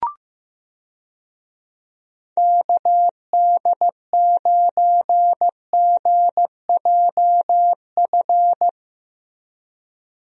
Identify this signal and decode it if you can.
Morse code, heard as a keyed tone, about 15 wpm, 700 Hz. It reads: KD9GJF
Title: USA Call Signs - 6 Characters - Repeated 2x - 15wpm